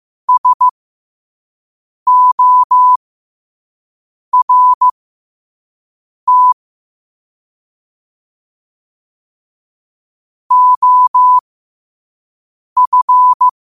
Morse code, heard as a keyed tone, about 15 wpm, spaced as for 5 wpm, 1000 Hz. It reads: SORT OF
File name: qotdfiles / 5wpm